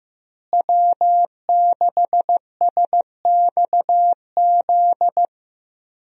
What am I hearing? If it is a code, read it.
Morse code, heard as a keyed tone, about 15 wpm, 700 Hz. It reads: W6SXZ
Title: USA Call Signs - 5 Characters - Repeated 3x - 15wpm